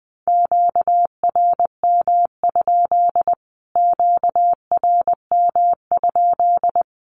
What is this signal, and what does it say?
Morse code, heard as a keyed tone, about 20 wpm, 700 Hz. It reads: QRM? QRM?
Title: Q-Codes Repeated 3x 20wpm